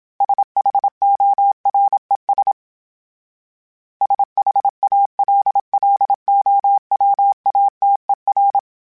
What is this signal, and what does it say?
Morse code, heard as a keyed tone, about 27 wpm, 800 Hz. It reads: SHORES SHALLOWATER
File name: fromSMs / FromSMs mp3s / Cities@25